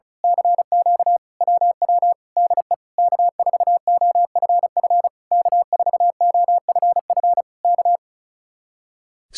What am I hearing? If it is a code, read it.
Morse code, heard as a keyed tone, about 32 wpm, 700 Hz. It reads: CQ WW DE K4OFF K4OFF K
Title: Speed-Racing Common QSO Phrases Encore - 35wpm